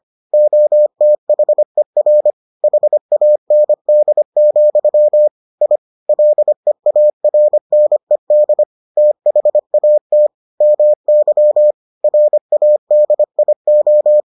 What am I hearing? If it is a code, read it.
Morse code, heard as a keyed tone, about 25 wpm, 600 Hz. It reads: OTHER HAND, I LEARNED THAT MY RADIO